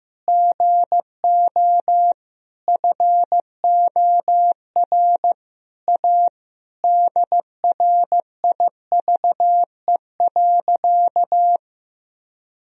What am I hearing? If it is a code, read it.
Morse code, heard as a keyed tone, about 15 wpm, 700 Hz. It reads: GO FOR A DRIVE.
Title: Sentences from Top 1000 Words - Review Part 2 - 15wpm